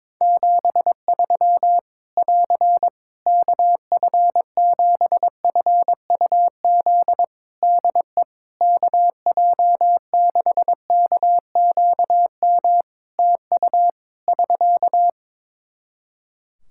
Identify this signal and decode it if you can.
Morse code, heard as a keyed tone, about 22 wpm, 700 Hz. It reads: 73 <AR> KF7FUZ DE KJ6KQM TU <SK>
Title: Practice QSO - Exchange 4 - Ending - 22wpm